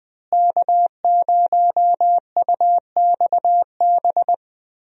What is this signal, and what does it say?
Morse code, heard as a keyed tone, about 20 wpm, 700 Hz. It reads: K0UXB
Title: USA Call Signs - 5 Characters - Repeated 2x - 20wpm